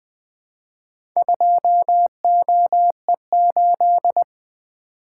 Morse code, heard as a keyed tone, about 20 wpm, 700 Hz. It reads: 2OE8
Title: Four Letter-Number Combinations - 20wpm